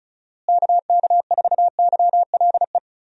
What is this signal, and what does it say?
Morse code, heard as a keyed tone, about 36 wpm, 700 Hz. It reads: KK4YLE
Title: USA Call Signs - 6 Characters - Repeated 2x - 35wpm